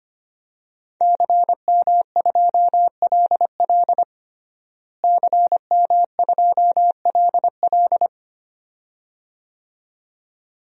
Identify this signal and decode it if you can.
Morse code, heard as a keyed tone, about 25 wpm, 700 Hz. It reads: CM2LL CM2LL
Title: DXCC-100 Call Signs - 5 Characters - Repeated 2x - 25wpm